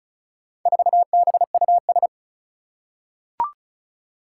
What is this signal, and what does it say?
Morse code, heard as a keyed tone, about 36 wpm, 700 Hz. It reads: 4BUS
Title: Four Challenging Letter-Number Combinations 35wpm